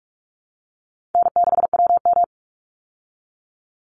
Morse code, heard as a keyed tone, about 44 wpm, 700 Hz. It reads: N6WM
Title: Call Signs Repeated 2x - 45wpm